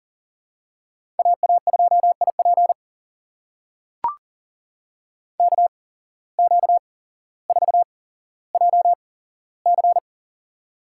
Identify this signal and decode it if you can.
Morse code, heard as a keyed tone, about 38 wpm, spaced as for 17 wpm, 700 Hz. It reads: AA2IP KQ4JC
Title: USA Call Signs - 5 Characters - 8x Character Spacing - 40wpm